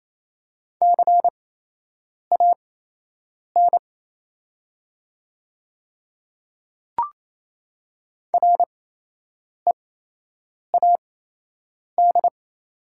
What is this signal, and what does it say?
Morse code, heard as a keyed tone, about 28 wpm, spaced as for 8 wpm, 700 Hz. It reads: CAN READ
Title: 8x Character Spacing - Top 200 Words - 28wpm